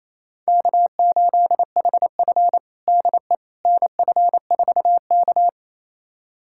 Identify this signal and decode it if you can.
Morse code, heard as a keyed tone, about 28 wpm, 700 Hz. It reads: K8HF DE NF4K
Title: USA Callsign DE Callsign - 4 Characters - 28wpm